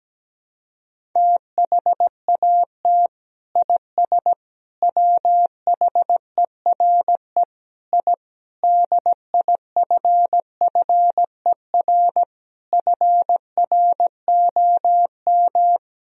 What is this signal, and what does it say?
Morse code, heard as a keyed tone, about 17 wpm, 700 Hz. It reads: THAT IS WHERE I DIFFER FROM
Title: Sentences from Top 1000 Words - Review Part 1 - 17wpm